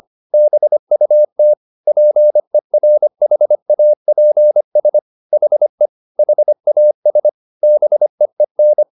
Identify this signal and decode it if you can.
Morse code, heard as a keyed tone, about 25 wpm, 600 Hz. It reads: BUT PERHAPS HE HAS BEEN
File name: the old man and the sea part 1 25wpm